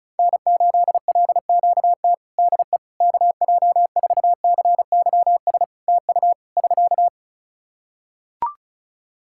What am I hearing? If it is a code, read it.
Morse code, heard as a keyed tone, about 34 wpm, 700 Hz. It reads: N8LQT DE KJ4CYS TU <SK>
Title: K1USN SST QSO Practice 35wpm